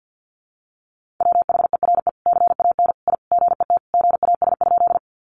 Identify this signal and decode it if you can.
Morse code, heard as a keyed tone, about 50 wpm, 700 Hz. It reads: WHERE CAN I GET GAS?
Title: Sentences from Top 1100 Words 50wpm